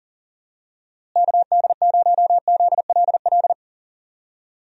Morse code, heard as a keyed tone, about 40 wpm, 700 Hz. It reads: KD0ZLL
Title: USA Call Signs - 6 Characters - Repeated 2x - 40wpm